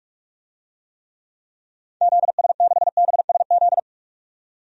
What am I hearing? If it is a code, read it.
Morse code, heard as a keyed tone, about 46 wpm, 700 Hz. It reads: ZS6BSZ
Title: DXCC-100 Call Signs - 6 Characters - 45wpm